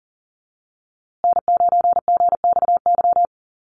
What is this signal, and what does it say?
Morse code, heard as a keyed tone, about 40 wpm, 700 Hz. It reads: N9GXY
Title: USA Call Signs Repeated 3x 40wpm